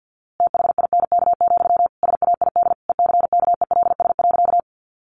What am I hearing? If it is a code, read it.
Morse code, heard as a keyed tone, about 50 wpm, 700 Hz. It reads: THINK, SAID ECKELS.
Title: A Sound Of Thunder 50wpm